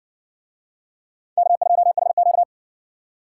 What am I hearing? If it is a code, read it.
Morse code, heard as a keyed tone, about 60 wpm, 700 Hz. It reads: K2HQ